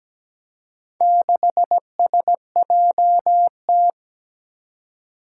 Morse code, heard as a keyed tone, about 17 wpm, 700 Hz. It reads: 6SJT
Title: Four Challenging Letter-Number Combinations 17wpm